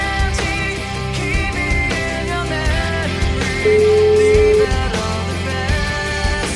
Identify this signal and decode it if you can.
Ringback tone (Europe, 425 Hz), with noise.